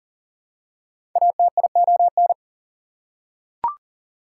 Morse code, heard as a keyed tone, about 40 wpm, 700 Hz. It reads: ATION
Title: Prefixes and Suffixes 40wpm